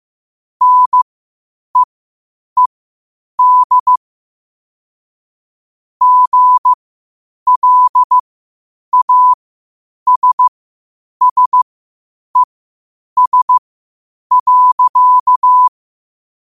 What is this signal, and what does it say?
Morse code, heard as a keyed tone, about 15 wpm, spaced as for 8 wpm, 1000 Hz. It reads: NEED GLASSES.
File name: qotdfiles / 8wpm